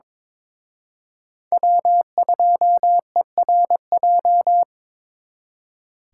Morse code, heard as a keyed tone, about 26 wpm, 700 Hz. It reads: W2ERJ
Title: Speed Racing - USA 1x3 Call Signs 22wpm